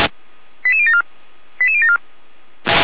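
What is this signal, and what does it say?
Five-tone selective call (ZVEI), heard as quick runs of five tones, tones of about 70 ms. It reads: 88074 88074
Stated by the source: ZVEI five-tone selective call received on SDR, logged by feuerwehr-orlamuende fire brigade server.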